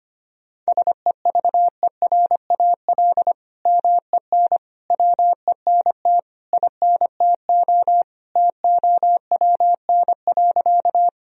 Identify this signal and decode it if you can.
Morse code, heard as a keyed tone, about 25 wpm, 700 Hz. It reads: SEVERAL MEN WENT INTO TOWN.
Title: Sentences from Top 1000 Words - Review Part 2 - 25wpm